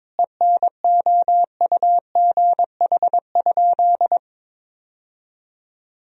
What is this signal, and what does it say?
Morse code, heard as a keyed tone, about 21 wpm, 700 Hz. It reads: ENOUGH?
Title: Sentences from Top 1000 Words - Review Part 3 - 22wpm